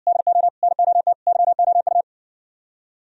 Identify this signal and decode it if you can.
Morse code, heard as a keyed tone, about 60 wpm, 700 Hz. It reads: DO NOT YOU